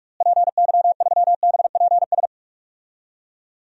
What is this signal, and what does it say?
Morse code, heard as a keyed tone, about 46 wpm, 700 Hz. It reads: PY3BPS
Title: DXCC-100 Call Signs - 6 Characters - Repeated 2x - 45wpm